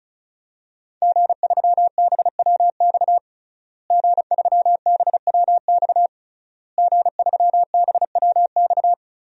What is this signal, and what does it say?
Morse code, heard as a keyed tone, about 36 wpm, 700 Hz. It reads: G3BWX G3BWX G3BWX
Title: DXCC-100 Call Signs - 5 Characters - Repeated 3x - 35wpm